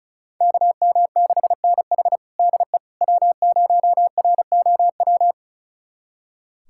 Morse code, heard as a keyed tone, about 36 wpm, 700 Hz. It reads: KM6NH DE W0ROW
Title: USA Callsign DE Callsign - 5 Characters - 35wpm